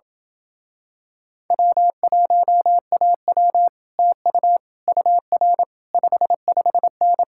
Morse code, heard as a keyed tone, about 28 wpm, 700 Hz. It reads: W1AW TU UR 55N